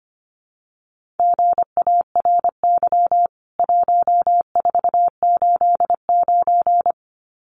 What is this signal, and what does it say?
Morse code, heard as a keyed tone, about 25 wpm, 700 Hz. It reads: GARY 1489
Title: CWT Exchange 25wpm